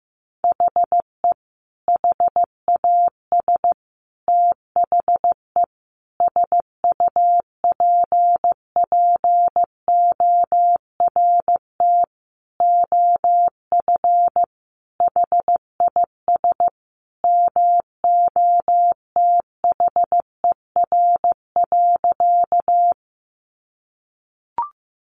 Morse code, heard as a keyed tone, about 14 wpm, 700 Hz. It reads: HE HAS THE SUPPORT OF HIS MOTHER.
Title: Sentences from Top 800 Words 15wpm